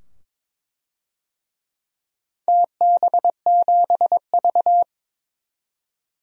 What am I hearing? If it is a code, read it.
Morse code, heard as a keyed tone, about 22 wpm, 700 Hz. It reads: TB7V
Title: DXCC-100 Call Signs - 4 Characters - 22wpm